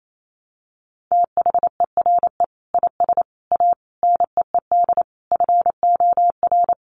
Morse code, heard as a keyed tone, about 28 wpm, 700 Hz. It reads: THERE IS A NEED FOR
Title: Sets of 5 Words Encore2 28wpm